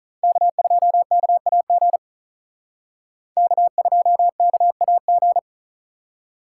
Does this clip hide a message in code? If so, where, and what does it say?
Yes: Morse code from 0.2 to 5.4 s, heard as a keyed tone, about 46 wpm, 700 Hz. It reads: K2KAG K2KAG